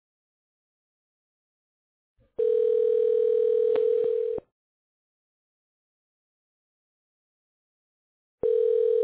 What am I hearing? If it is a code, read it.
Ringback tone (North America), 440 + 480 Hz.